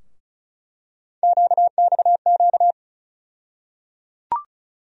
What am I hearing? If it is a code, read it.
Morse code, heard as a keyed tone, about 36 wpm, 700 Hz. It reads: QXQ
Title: USA Call Sign Postfixes - 3 Letters 35wpm